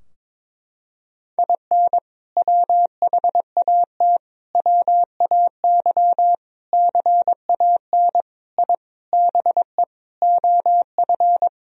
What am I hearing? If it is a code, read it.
Morse code, heard as a keyed tone, about 22 wpm, 700 Hz. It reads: IN WHAT WAY CAN I BE OF
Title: Sentences from Top 1000 Words - Review Part 4 - 22wpm